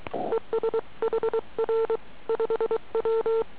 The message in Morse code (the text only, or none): ES HR 5W